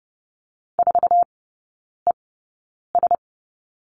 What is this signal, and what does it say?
Morse code, heard as a keyed tone, about 30 wpm, spaced as for 9 wpm, 700 Hz. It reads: 4ES
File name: Lesson 15e - Number 5 - Three Characters 30wpm